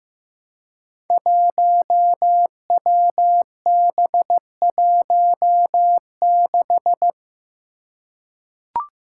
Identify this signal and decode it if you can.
Morse code, heard as a keyed tone, about 15 wpm, 700 Hz. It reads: 1WB16
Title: Five Challenging Letter-Number Combinations 15wpm